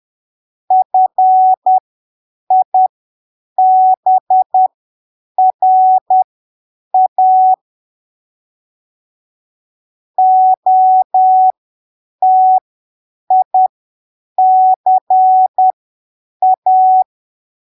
Morse code, heard as a keyed tone, about 10 wpm, spaced as for 7 wpm, 750 Hz. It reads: FIBRA OTICA
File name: arquivos / texto / texto 11 10ppm